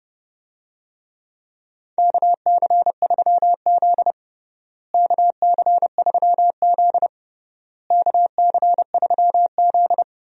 Morse code, heard as a keyed tone, about 31 wpm, 700 Hz. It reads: KC3Z KC3Z KC3Z
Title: USA Call Signs - 4 Characters - Repeated 3x - 30wpm